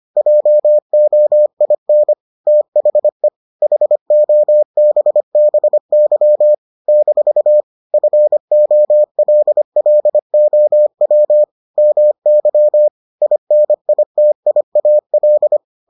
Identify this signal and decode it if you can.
Morse code, heard as a keyed tone, about 25 wpm, 600 Hz. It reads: JOIN THE HOBBY - FOLLOW MY INITIAL